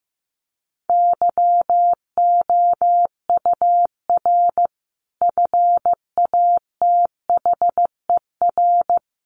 Morse code, heard as a keyed tone, about 15 wpm, 700 Hz. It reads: YOUR FATHER
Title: Sentences from Top 1400 Words 15wpm